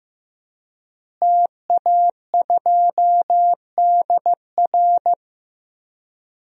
Morse code, heard as a keyed tone, about 15 wpm, 700 Hz. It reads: TA2DR
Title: DXCC-100 Call Signs - 5 Characters - Repeated 2x - 15wpm